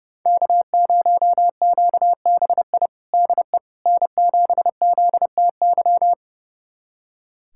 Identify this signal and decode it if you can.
Morse code, heard as a keyed tone, about 30 wpm, 700 Hz. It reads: K0QBI DE N7ZTY
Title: USA Callsign DE Callsign - 5 Characters - 30wpm